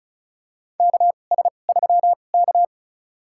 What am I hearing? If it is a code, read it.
Morse code, heard as a keyed tone, about 36 wpm, spaced as for 25 wpm, 700 Hz. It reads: KS3K
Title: USA Call Signs - 2x Character Spacing - 35wpm